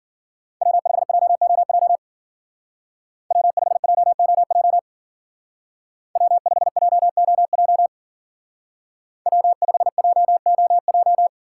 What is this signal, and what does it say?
Morse code, heard as a keyed tone, about 53 wpm, 700 Hz. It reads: W5JOJ W5JOJ W5JOJ W5JOJ